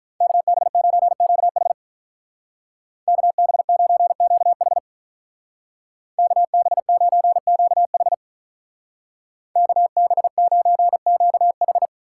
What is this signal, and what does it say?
Morse code, heard as a keyed tone, about 46 wpm, 700 Hz. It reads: KB9QH KB9QH KB9QH KB9QH